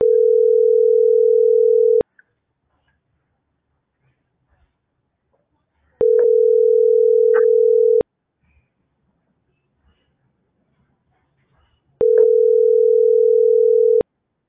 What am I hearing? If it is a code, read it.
Ringback tone (North America), 440 + 480 Hz.